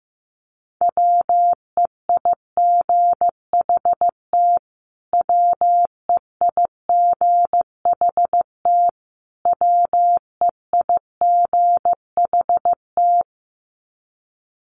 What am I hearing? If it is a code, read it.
Morse code, heard as a keyed tone, about 15 wpm, 700 Hz. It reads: WEIGHT WEIGHT WEIGHT
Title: Top 500 Words Repeated 15wpm